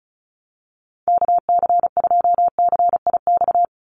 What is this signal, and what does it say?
Morse code, heard as a keyed tone, about 34 wpm, 700 Hz. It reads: KC2CIX
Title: USA Call Signs 35wpm